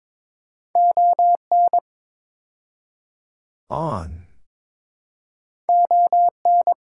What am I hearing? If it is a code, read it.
Morse code, heard as a keyed tone, about 22 wpm, 700 Hz. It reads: ON ON
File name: practice / 22wpm / Build A Phrase - 5-Word Phrases - 22wpm